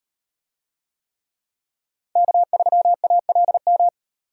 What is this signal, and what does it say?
Morse code, heard as a keyed tone, about 38 wpm, 700 Hz. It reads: K3ALM